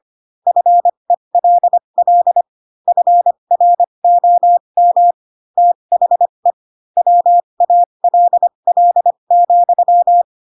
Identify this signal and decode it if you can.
Morse code, heard as a keyed tone, about 25 wpm, spaced as for 22 wpm, 700 Hz. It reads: FELL FROM THE WALL,
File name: Pinocchio / Pinocchio ch4 18 25 ik1hge